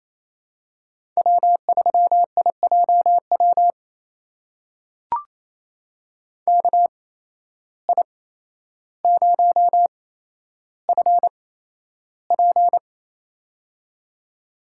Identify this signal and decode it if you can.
Morse code, heard as a keyed tone, about 28 wpm, spaced as for 12 wpm, 700 Hz. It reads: W3IJW KI0FP
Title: USA Call Signs - 5 Characters - 8x Character Spacing - 28wpm